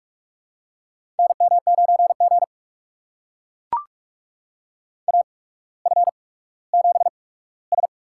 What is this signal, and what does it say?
Morse code, heard as a keyed tone, about 44 wpm, spaced as for 19 wpm, 690 Hz. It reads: NM9G AF7S
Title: USA Call Signs - 4 Characters - 8x Character Spacing - 45wpm